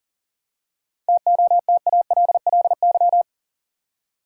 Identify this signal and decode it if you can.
Morse code, heard as a keyed tone, about 40 wpm, 700 Hz. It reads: TOTALLY
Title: Seven Letter Words - 40wpm